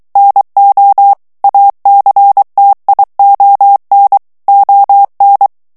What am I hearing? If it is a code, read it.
Morse code, heard as a keyed tone, about 23 wpm, 800 Hz. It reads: NO ACTION ON
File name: ANGLAIS texte clair 3 vitesse 22 mots minute